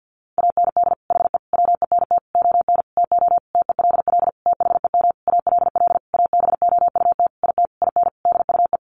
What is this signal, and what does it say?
Morse code, heard as a keyed tone, about 49 wpm, 700 Hz. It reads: AND HE WENT ON TO TELL THEM ALL ABOUT IT IN DUE
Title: Sentences from Top 900 Words 50wpm